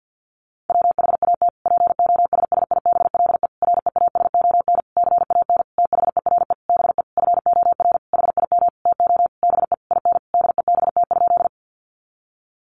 Morse code, heard as a keyed tone, about 50 wpm, 700 Hz. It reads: WHAT POSSIBLE REASON CAN THERE BE FOR HIM TO BE IN DEBT?
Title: Sentences from Top 1300 Words 50wpm